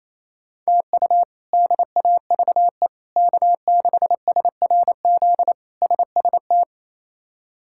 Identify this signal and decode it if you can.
Morse code, heard as a keyed tone, about 27 wpm, 700 Hz. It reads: TU DAVE K6SRZ SST
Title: K1USN SST QSO Practice - 28wpm